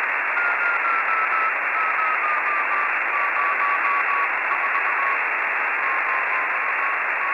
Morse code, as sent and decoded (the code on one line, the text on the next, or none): none